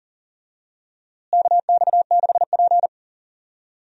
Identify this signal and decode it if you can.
Morse code, heard as a keyed tone, about 40 wpm, 700 Hz. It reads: KX6P
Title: USA Call Signs - 4 Characters - Repeated 3x - 40wpm